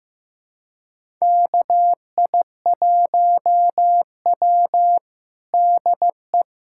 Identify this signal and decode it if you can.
Morse code, heard as a keyed tone, about 16 wpm, 700 Hz. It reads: KI1W DE
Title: USA Callsign DE Callsign - 4 Characters - 15wpm